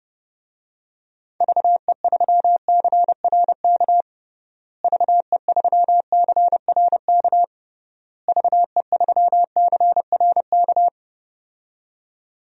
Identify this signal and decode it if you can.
Morse code, heard as a keyed tone, about 32 wpm, 700 Hz. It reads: VE3CRK VE3CRK VE3CRK
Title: DXCC-100 Call Signs - 6 Characters - Repeated 3x - 30wpm